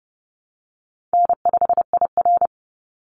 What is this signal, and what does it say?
Morse code, heard as a keyed tone, about 30 wpm, 700 Hz. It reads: N5IR
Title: Easy Call Signs - 30wpm